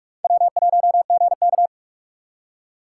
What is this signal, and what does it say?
Morse code, heard as a keyed tone, about 43 wpm, 700 Hz. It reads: W1GK